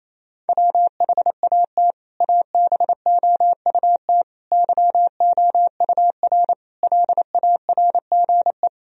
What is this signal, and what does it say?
Morse code, heard as a keyed tone, about 28 wpm, 700 Hz. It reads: WHAT ABOUT YOUR LARGE
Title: Sentences from Top 1000 Words - Review Part 1 - 28wpm